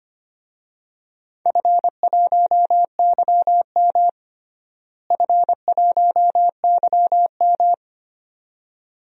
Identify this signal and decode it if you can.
Morse code, heard as a keyed tone, about 25 wpm, 700 Hz. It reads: F1YM F1YM
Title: DXCC-100 Call Signs - 4 Characters - Repeated 2x - 25wpm